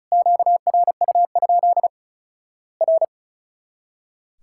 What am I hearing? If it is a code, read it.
Morse code, heard as a keyed tone, about 34 wpm, 700 Hz. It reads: QRU? R
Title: Worked All Europe DX Contest - QTC Exchange - 35wpm